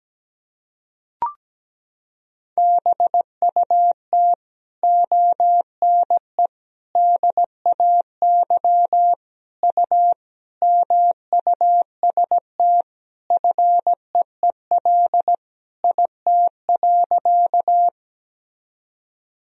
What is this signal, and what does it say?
Morse code, heard as a keyed tone, about 17 wpm, 700 Hz. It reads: BUT ONE DAY U MUST FEEL IT.
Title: Sentences from Top 400 Words and Abbreviations 17wpm